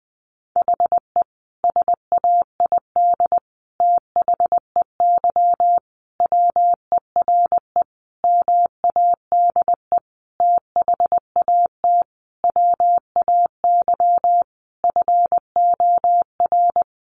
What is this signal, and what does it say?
Morse code, heard as a keyed tone, about 20 wpm, 700 Hz. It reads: HE SAID THEY WERE MADE THAT WAY FOR
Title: Sentences from Top 700 Words 20wpm